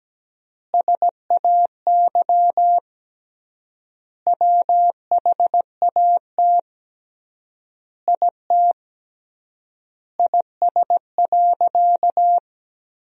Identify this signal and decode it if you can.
Morse code, heard as a keyed tone, about 17 wpm, 700 Hz. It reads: SAY WHAT IT IS.
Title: 3x Word-Spacing - Sentences from Top 200 Words - 17wpm